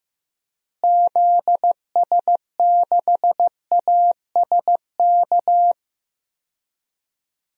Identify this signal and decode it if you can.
Morse code, heard as a keyed tone, about 15 wpm, 700 Hz. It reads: ZS6ASK
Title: DXCC-100 Call Signs - 6 Characters - Repeated 3x - 15wpm